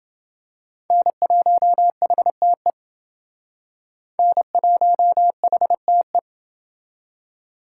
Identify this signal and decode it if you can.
Morse code, heard as a keyed tone, about 28 wpm, 700 Hz. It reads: N1HTE N1HTE